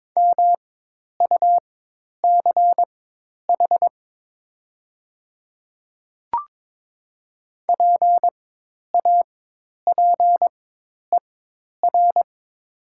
Morse code, heard as a keyed tone, about 22 wpm, spaced as for 10 wpm, 700 Hz. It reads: MUCH PAPER
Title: 4x Character Spacing - Top 300 Words - 22wpm